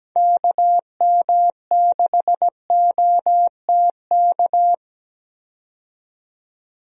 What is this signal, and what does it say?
Morse code, heard as a keyed tone, about 17 wpm, 700 Hz. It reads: KM6OTK
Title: USA Call Signs - 6 Characters - 17wpm